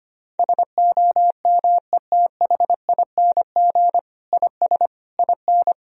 Morse code, heard as a keyed tone, about 25 wpm, 700 Hz. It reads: SOMETHING IS IN
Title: Sentences from Top 1000 Words - Review Part 1 - 25wpm